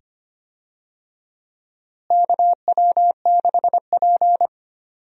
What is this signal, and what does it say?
Morse code, heard as a keyed tone, about 25 wpm, 700 Hz. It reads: KW6P
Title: USA Call Signs - 4 Characters - Repeated 2x - 25wpm